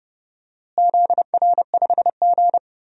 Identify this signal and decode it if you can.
Morse code, heard as a keyed tone, about 30 wpm, 700 Hz. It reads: ZR5G
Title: DXCC-100 Call Signs - 4 Characters - Repeated 3x - 30wpm